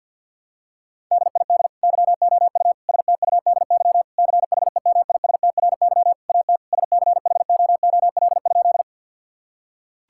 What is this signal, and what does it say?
Morse code, heard as a keyed tone, about 50 wpm, 700 Hz. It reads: DID YOU STUDY CHEMISTRY AT SCHOOL?
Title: Sentences from Top 1000 Words - Review Part 6 - 50wpm